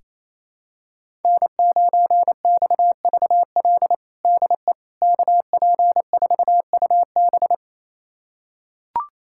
Morse code, heard as a keyed tone, about 28 wpm, 700 Hz. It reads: N9XVL DE KP4UB
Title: USA Callsign DE Callsign 28wpm